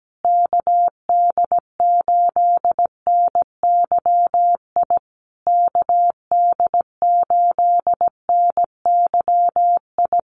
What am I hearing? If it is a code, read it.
Morse code, heard as a keyed tone, about 17 wpm, 700 Hz. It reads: KD8NYI KD8NYI
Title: USA Call Signs Repeated 3x 17wpm